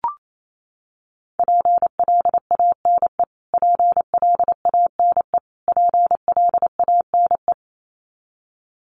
Morse code, heard as a keyed tone, about 28 wpm, 700 Hz. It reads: PLANE PLANE PLANE
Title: Top 500 Words Repeated 28wpm